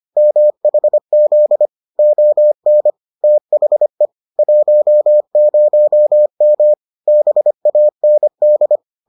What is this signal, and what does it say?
Morse code, heard as a keyed tone, about 25 wpm, 600 Hz. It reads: MHZ ON THE 10M BAND